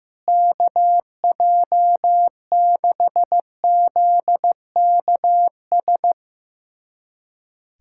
Morse code, heard as a keyed tone, about 16 wpm, 700 Hz. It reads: KJ6ZKS
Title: USA Callsign DE Callsign - 6 Characters - 15wpm